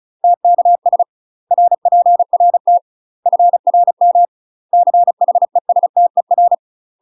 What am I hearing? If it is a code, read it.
Morse code, heard as a keyed tone, about 36 wpm, 700 Hz. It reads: TKS RPRT FRM CHESTER
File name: UK MASTER QSOs TRAINING / 35WPM / UK MASTER QSOs 0012